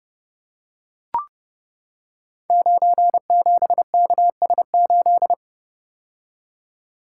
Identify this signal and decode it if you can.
Morse code, heard as a keyed tone, about 30 wpm, 700 Hz. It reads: 97KS8
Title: Code Groups - 5 Letter:Numbers - 30wpm